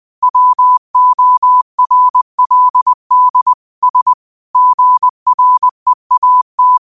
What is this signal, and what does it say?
Morse code, heard as a keyed tone, about 20 wpm, 1000 Hz. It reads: WORLD S GREAT
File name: qotdfiles / 20wpm